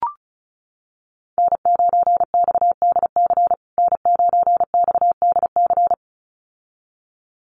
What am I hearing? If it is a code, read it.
Morse code, heard as a keyed tone, about 34 wpm, 700 Hz. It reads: N9XDC N9XDC
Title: USA Call Signs Repeated 2x 35wpm